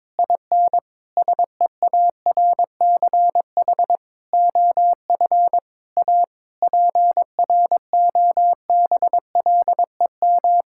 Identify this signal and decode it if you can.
Morse code, heard as a keyed tone, about 22 wpm, 700 Hz. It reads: IN SEARCH OF A PROBLEM
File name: practice / 22wpm / Sets of 7 Words - 22wpm